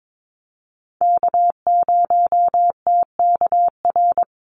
Morse code, heard as a keyed tone, about 22 wpm, 700 Hz. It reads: K0TKR
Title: USA Call Signs Repeated 2x 22wpm